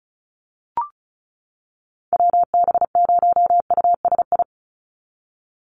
Morse code, heard as a keyed tone, about 36 wpm, 700 Hz. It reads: WB0USI
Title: NA SOTA Activator Callsigns 35wpm